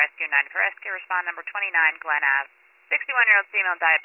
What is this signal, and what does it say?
Emergency Alert System (SAME) tones, heard as harsh buzzing bursts, with speech. It is heard from 0.5 to 3.4 s.